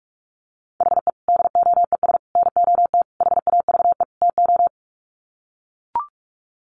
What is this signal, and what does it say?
Morse code, heard as a keyed tone, about 44 wpm, 700 Hz. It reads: HE DOES NOT HAVE TO
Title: Sets of 5 Words Encore 45wpm